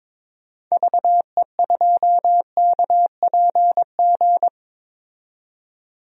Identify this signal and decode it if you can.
Morse code, heard as a keyed tone, about 22 wpm, 700 Hz. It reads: VE2KPG
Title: DXCC-100 Call Signs - 6 Characters - 22wpm